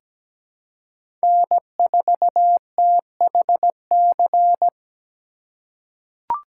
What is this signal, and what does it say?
Morse code, heard as a keyed tone, about 17 wpm, 700 Hz. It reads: N4THC
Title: USA Call Signs - 5 Characters - 3x Character Spacing - 17wpm